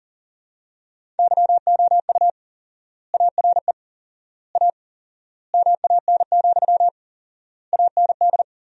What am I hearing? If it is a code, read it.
Morse code, heard as a keyed tone, about 40 wpm, 700 Hz. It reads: YOU ARE A MAN, AND
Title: 4x Word-Spacing - Sentences from Top 200 Words - 40wpm